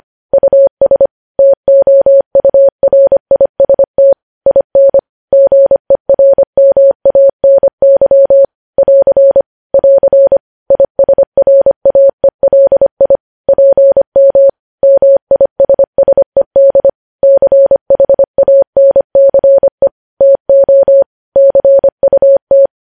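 Morse code, heard as a keyed tone, about 25 wpm, 550 Hz. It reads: US TOURIST IN GERMANY <AR> <AR> ISRAELI PM MISSED CHANCE TO CUT
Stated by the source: Morse code news headlines sent at 25 wpm.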